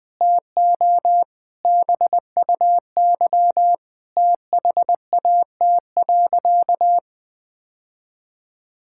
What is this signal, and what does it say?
Morse code, heard as a keyed tone, about 20 wpm, 700 Hz. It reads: TO BUY THAT.
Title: Sentences with 5-Word Phrases - 20wpm